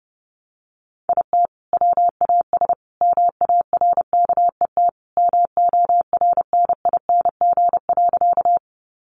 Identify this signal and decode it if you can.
Morse code, heard as a keyed tone, about 30 wpm, 700 Hz. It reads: IT WAS MARKET MORNING.
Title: Sentences from Top 600 Words 30wpm